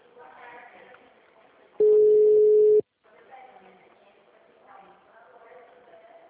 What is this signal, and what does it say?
Ringback tone (Europe, 425 Hz), with speech.
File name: out- -2013- - - .43647